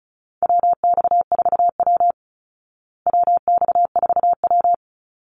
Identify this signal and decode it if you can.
Morse code, heard as a keyed tone, about 34 wpm, 700 Hz. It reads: WX4W WX4W
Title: Call Signs Repeated 3x - 35wpm